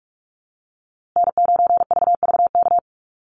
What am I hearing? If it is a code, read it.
Morse code, heard as a keyed tone, about 44 wpm, 690 Hz. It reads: N9VVK
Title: USA Call Signs Repeated 2x 45wpm